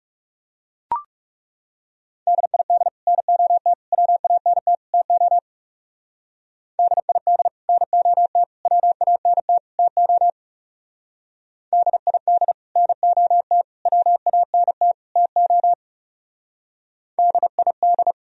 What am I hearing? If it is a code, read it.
Morse code, heard as a keyed tone, about 39 wpm, 700 Hz. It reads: DID NOT WANT TO DID NOT WANT TO DID NOT WANT TO DID